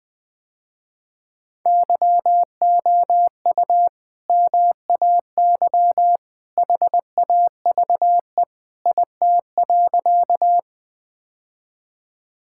Morse code, heard as a keyed tone, about 20 wpm, 700 Hz. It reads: YOU MAY HAVE IT.
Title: Sentences with 2-Word Phrases - Spoken Beginning - 20wpm